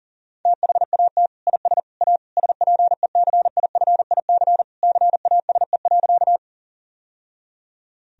Morse code, heard as a keyed tone, about 40 wpm, 700 Hz. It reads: THAT IS A SPECIFIC CASE.
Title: Sentences from Top 1000 Words - Review Part 3 - 40wpm